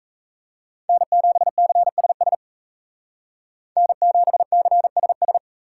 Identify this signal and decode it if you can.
Morse code, heard as a keyed tone, about 36 wpm, 700 Hz. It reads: N7CSS N7CSS